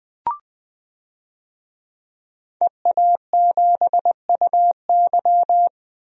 Morse code, heard as a keyed tone, about 20 wpm, 700 Hz. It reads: EA7UY
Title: DXCC-100 Call Signs - 5 Characters - 20wpm